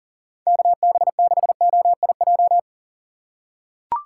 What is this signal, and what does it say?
Morse code, heard as a keyed tone, about 40 wpm, 700 Hz. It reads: KB6OIJ
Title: USA Call Signs - 6 Characters - 6x Character Spacing - 40wpm